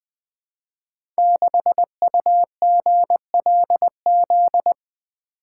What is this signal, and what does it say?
Morse code, heard as a keyed tone, about 20 wpm, 700 Hz. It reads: 6UGLZ
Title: Code Groups - 5 Letter:Numbers - 20wpm